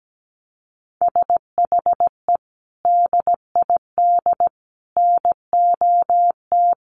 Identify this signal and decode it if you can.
Morse code, heard as a keyed tone, about 17 wpm, 700 Hz. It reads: SHE DID NOT
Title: Sets of 4 Words 17wpm